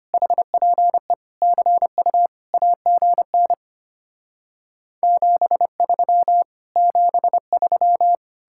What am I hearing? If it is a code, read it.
Morse code, heard as a keyed tone, about 33 wpm, 700 Hz. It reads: HPE CU AGN 73 73